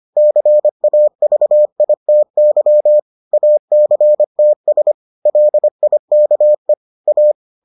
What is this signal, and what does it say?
Morse code, heard as a keyed tone, about 25 wpm, 600 Hz. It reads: CAVITY ACTS LIKE A